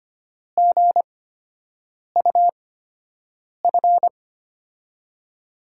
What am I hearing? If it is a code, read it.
Morse code, heard as a keyed tone, about 25 wpm, spaced as for 7 wpm, 700 Hz. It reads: GUF
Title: USA Call Signs - 8x Character Spacing - 25wpm